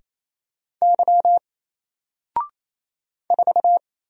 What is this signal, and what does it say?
Morse code, heard as a keyed tone, about 27 wpm, 700 Hz. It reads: Y 4
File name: Single-Letter-Number-28wpm